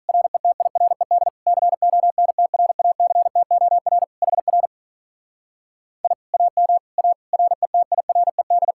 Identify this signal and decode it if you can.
Morse code, heard as a keyed tone, about 46 wpm, 700 Hz. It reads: RETIRED CONTRACTOR HR I AM A RETIRED